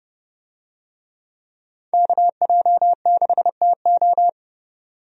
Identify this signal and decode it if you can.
Morse code, heard as a keyed tone, about 30 wpm, 700 Hz. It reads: KJ6TO
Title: USA Call Signs - 5 Characters - 30wpm